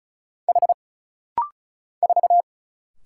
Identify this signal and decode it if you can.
Morse code, heard as a keyed tone, about 34 wpm, 700 Hz. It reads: H 4
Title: Single Letter-Number - Rapid-Fire 35wpm